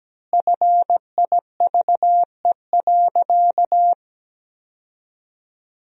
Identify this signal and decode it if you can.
Morse code, heard as a keyed tone, about 17 wpm, 700 Hz. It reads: FIVE.
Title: 4x Word-Spacing - Sentences from Top 400 Words - 17wpm